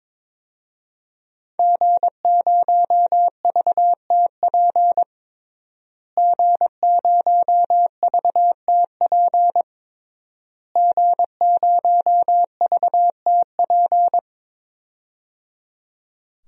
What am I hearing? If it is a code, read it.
Morse code, heard as a keyed tone, about 22 wpm, 700 Hz. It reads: G0VTP G0VTP G0VTP
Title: DXCC-100 Call Signs - 5 Characters - Repeated 3x - 22wpm